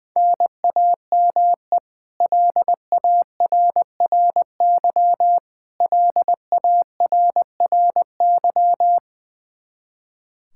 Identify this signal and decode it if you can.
Morse code, heard as a keyed tone, about 20 wpm, 700 Hz. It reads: NAME LARRY LARRY
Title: Practice QSO - Exchange 1 - Beginning - 20wpm